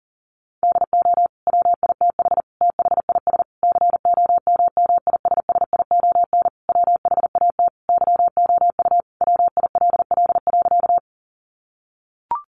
Morse code, heard as a keyed tone, about 37 wpm, 700 Hz. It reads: DO WITH THIS COMMISSION WHAT YOU WILL.
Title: Sentences from Top 1400 Words 40wpm